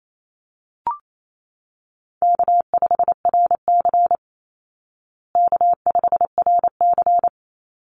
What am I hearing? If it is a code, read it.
Morse code, heard as a keyed tone, about 28 wpm, 700 Hz. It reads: K5RC K5RC
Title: Easy Call Signs Repeated 2x - 28wpm